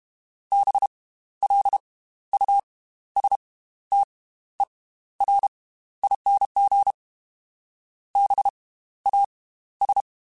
Morse code, heard as a keyed tone, about 32 wpm, spaced as for 14 wpm, 800 Hz. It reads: BLUSTERING BAS
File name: Slow ing @30